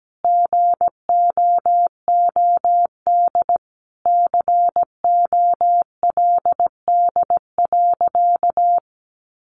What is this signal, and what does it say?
Morse code, heard as a keyed tone, about 17 wpm, 700 Hz. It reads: GOOD COLD.
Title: Sentences from Top 1300 Words 17wpm